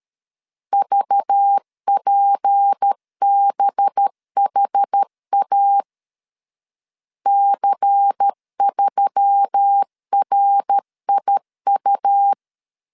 Morse code, heard as a keyed tone, about 13 wpm, 800 Hz. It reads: VPBHA C3RIU
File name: CW / DET / 1 6 1500C S B 12wpm 3431